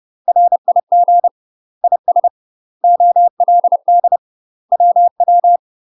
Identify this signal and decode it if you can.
Morse code, heard as a keyed tone, about 30 wpm, 700 Hz. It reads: RIG IS OLD WW
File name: UK MASTER QSOs TRAINING / 30WPM / UK MASTER QSOs 0015